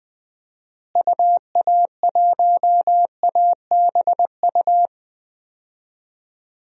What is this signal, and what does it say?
Morse code, heard as a keyed tone, about 21 wpm, 700 Hz. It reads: UA1ABU
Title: DXCC-100 Call Signs - 6 Characters - Repeated 2x - 20wpm